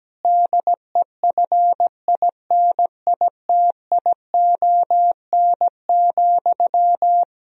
Morse code, heard as a keyed tone, about 17 wpm, 700 Hz. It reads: DEFINITION,
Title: Sentences from Top 1000 Words - Review Part 4 - 17wpm